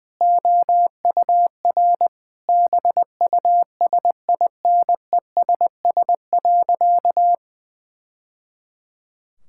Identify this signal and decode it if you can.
Morse code, heard as a keyed tone, about 20 wpm, 700 Hz. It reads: OUR BUSINESS.
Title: Sentences from Top 1000 Words - Review Part 6 - 20wpm